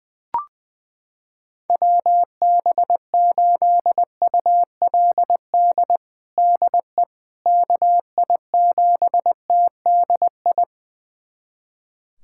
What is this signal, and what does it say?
Morse code, heard as a keyed tone, about 20 wpm, 700 Hz. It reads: WB8ULD DE KI7TDI
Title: USA Callsign DE Callsign - 6 Characters - 20wpm